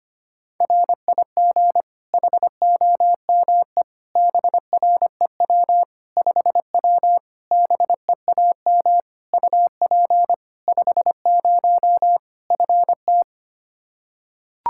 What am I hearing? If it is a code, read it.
Morse code, heard as a keyed tone, about 25 wpm, 700 Hz. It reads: RIG HOME BREW 5W BEAM UP 50 FT
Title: Practice QSO - Exchange 2 - Info - 25wpm